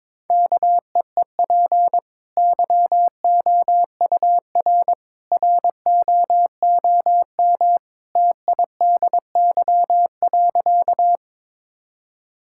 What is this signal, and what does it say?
Morse code, heard as a keyed tone, about 22 wpm, 700 Hz. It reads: KEEP YOUR ROOM TIDY.
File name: practice / 22wpm / Easy Sentences with Hints - 22wpm